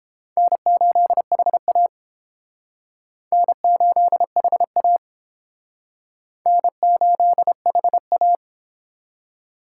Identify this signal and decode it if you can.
Morse code, heard as a keyed tone, about 31 wpm, 700 Hz. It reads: N8HA N8HA N8HA